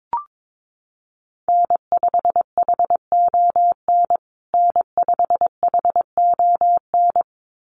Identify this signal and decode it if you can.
Morse code, heard as a keyed tone, about 22 wpm, 700 Hz. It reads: N5HON N5HON
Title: USA Call Signs Repeated 2x 22wpm